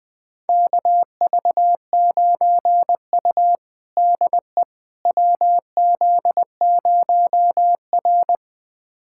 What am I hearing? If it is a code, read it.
Morse code, heard as a keyed tone, about 21 wpm, 700 Hz. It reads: KV9U DE WZ0R
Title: USA Callsign DE Callsign - 4 Characters - 20wpm